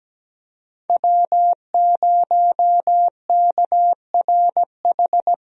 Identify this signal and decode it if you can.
Morse code, heard as a keyed tone, about 17 wpm, 700 Hz. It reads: W0KRH
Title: USA Call Signs - 5 Characters - Repeated 2x - 17wpm